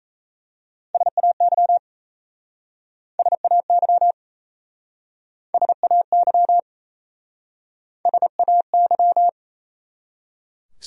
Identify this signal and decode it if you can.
Morse code, heard as a keyed tone, about 37 wpm, 700 Hz. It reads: SAY SAY SAY SAY